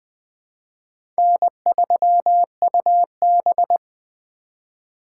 Morse code, heard as a keyed tone, about 20 wpm, 700 Hz. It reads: N3UB
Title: USA Call Signs - 4 Characters - 4x Character Spacing - 20wpm